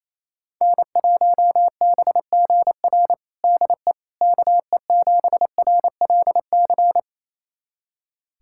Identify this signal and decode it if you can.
Morse code, heard as a keyed tone, about 28 wpm, 700 Hz. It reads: N1BGR DE KE7RLC
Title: USA Callsign DE Callsign 28wpm